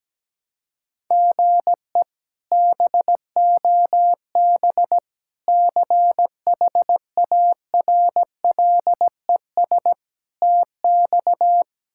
Morse code, heard as a keyed tone, about 17 wpm, 700 Hz. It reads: GE BOB CHARLES TX
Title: K1USN SST QSO Practice - 17wpm